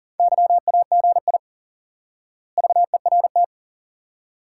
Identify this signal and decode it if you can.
Morse code, heard as a keyed tone, about 40 wpm, 700 Hz. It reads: YAGI VERT
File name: IQE Course 3 - Lesson 89 - YAGI - Two Elements - 40wpm